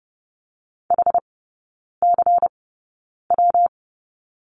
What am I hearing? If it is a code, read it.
Morse code, heard as a keyed tone, about 30 wpm, spaced as for 9 wpm, 700 Hz. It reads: HCW
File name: Lesson 19e - Letter w - Three Characters 30wpm